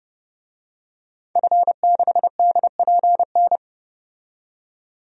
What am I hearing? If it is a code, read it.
Morse code, heard as a keyed tone, about 32 wpm, 700 Hz. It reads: F6DPN
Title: DXCC-100 Call Signs - 5 Characters - 30wpm